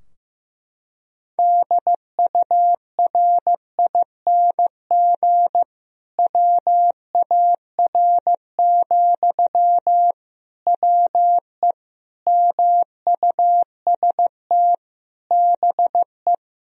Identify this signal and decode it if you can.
Morse code, heard as a keyed tone, about 16 wpm, 700 Hz. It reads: DURING WAR, WE MUST BE
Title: Sentences from Top 1000 Words - Review Part 4 - 15wpm